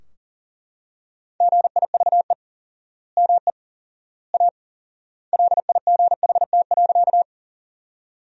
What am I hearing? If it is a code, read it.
Morse code, heard as a keyed tone, about 40 wpm, 700 Hz. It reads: GIVE ME A LIGHT.
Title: 4x Word-Spacing - Sentences from Top 200 Words - 40wpm